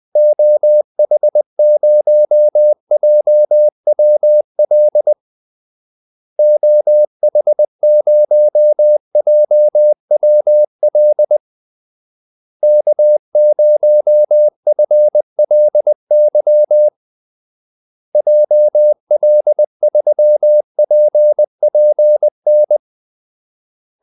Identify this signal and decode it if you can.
Morse code, heard as a keyed tone, about 20 wpm, 600 Hz. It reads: OH0JWL OH0JWL K0FLY JL3PPN
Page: Kaldesignaler 20 wpm